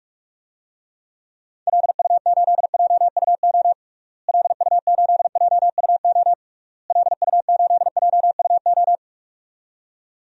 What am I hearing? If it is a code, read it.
Morse code, heard as a keyed tone, about 46 wpm, 700 Hz. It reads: LU8JUO LU8JUO LU8JUO
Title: DXCC-100 Call Signs - 6 Characters - Repeated 3x - 45wpm